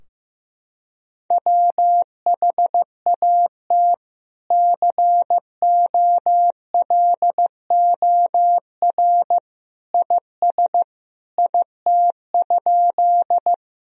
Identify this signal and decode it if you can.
Morse code, heard as a keyed tone, about 15 wpm, 700 Hz. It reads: WHAT COLOR IS IT?
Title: Sentences from Top 1000 Words - Review Part 2 - 15wpm